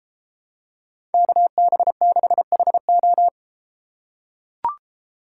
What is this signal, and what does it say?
Morse code, heard as a keyed tone, about 31 wpm, 700 Hz. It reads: KB6HO